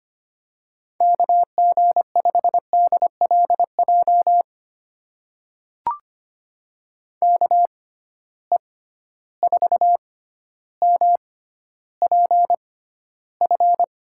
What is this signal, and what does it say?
Morse code, heard as a keyed tone, about 25 wpm, spaced as for 13 wpm, 700 Hz. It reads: KG5DLJ KE4MPF
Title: USA Call Signs - 6x Character Spacing - 25wpm